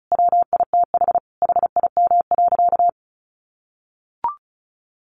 Morse code, heard as a keyed tone, about 34 wpm, 700 Hz. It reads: WITH HIM.
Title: Sentences from Top 1300 Words 35wpm